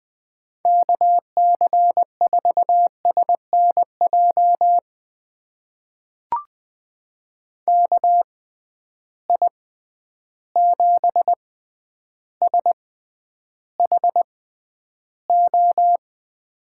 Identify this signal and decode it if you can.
Morse code, heard as a keyed tone, about 20 wpm, spaced as for 10 wpm, 700 Hz. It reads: KC4SNJ KI7SHO
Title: USA Call Signs - 6 Characters - 6x Character Spacing - 20wpm